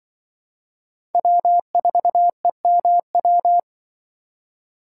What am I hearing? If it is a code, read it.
Morse code, heard as a keyed tone, about 24 wpm, 700 Hz. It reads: W4EMW
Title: Speed Racing - USA 1x3 Call Signs 20wpm